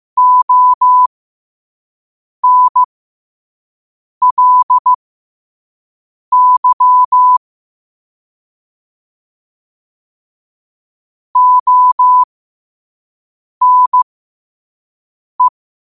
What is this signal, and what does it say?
Morse code, heard as a keyed tone, about 15 wpm, spaced as for 5 wpm, 1000 Hz. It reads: ONLY ONE
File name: qotdfiles / 5wpm